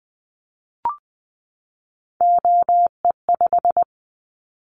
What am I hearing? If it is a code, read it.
Morse code, heard as a keyed tone, about 20 wpm, 700 Hz. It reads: OE5
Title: Callsign Prefixes for Top 25 DXCC - 20wpm